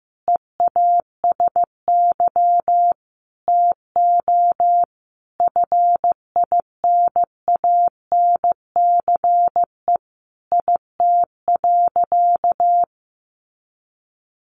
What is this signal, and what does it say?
Morse code, heard as a keyed tone, about 14 wpm, 700 Hz. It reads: EASY TO FINANCE IT.
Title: Sentences from Top 1500 Words 15wpm